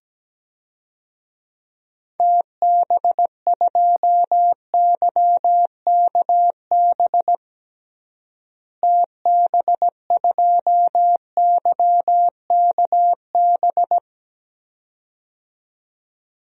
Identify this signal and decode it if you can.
Morse code, heard as a keyed tone, about 17 wpm, 700 Hz. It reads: TB2YKB TB2YKB
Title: DXCC-100 Call Signs - 6 Characters - Repeated 2x - 17wpm